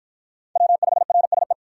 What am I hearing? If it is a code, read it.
Morse code, heard as a keyed tone, about 50 wpm, 700 Hz. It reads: W5RSE